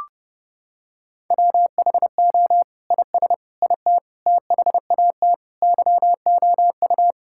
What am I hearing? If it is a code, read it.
Morse code, heard as a keyed tone, about 30 wpm, 700 Hz. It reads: WHO IS IT THAT YOU
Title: Sentences from Top 1600 Words 30wpm